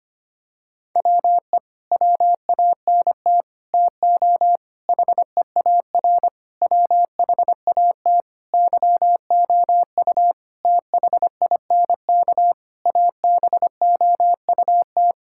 Morse code, heard as a keyed tone, about 25 wpm, 700 Hz. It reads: WE WANT TO HEAR WHAT YOU THINK ABOUT
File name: Sets of 9 Words - 25wpm